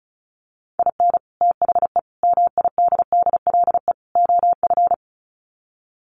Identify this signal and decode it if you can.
Morse code, heard as a keyed tone, about 34 wpm, 700 Hz. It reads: IN THE MIDDLE OF
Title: Sets of 4 Words 35wpm